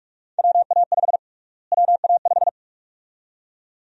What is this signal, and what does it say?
Morse code, heard as a keyed tone, about 46 wpm, 700 Hz. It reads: WA5 WA5
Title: USA Call Sign Prefixes - Repeated 2x - 45wpm